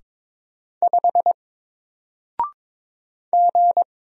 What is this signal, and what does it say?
Morse code, heard as a keyed tone, about 22 wpm, 700 Hz. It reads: 5 G
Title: Single Letter-Number Alt 22wpm